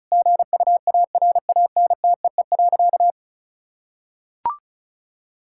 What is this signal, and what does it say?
Morse code, heard as a keyed tone, about 36 wpm, 700 Hz. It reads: GUARANTEE.
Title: Sentences from Top 1600 Words 35wpm